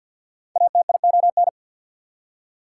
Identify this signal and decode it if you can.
Morse code, heard as a keyed tone, about 73 wpm, spaced as for 61 wpm, 700 Hz. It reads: ATION
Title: Speed-Racing Prefixes and Suffixes 50wpm